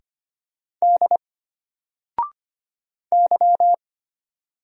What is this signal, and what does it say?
Morse code, heard as a keyed tone, about 25 wpm, 700 Hz. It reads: D Y
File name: Single-Letter-Number-25wpm